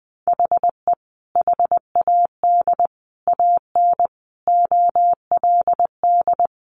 Morse code, heard as a keyed tone, about 20 wpm, 700 Hz. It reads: HE HAD AN OLD
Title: Sentences from Top 1100 Words 20wpm